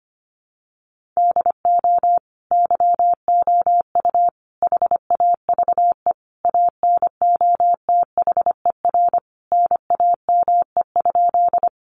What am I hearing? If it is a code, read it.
Morse code, heard as a keyed tone, about 25 wpm, 700 Hz. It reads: DO YOU HAVE ANOTHER NAME?
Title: Sentences from Top 600 Words 25wpm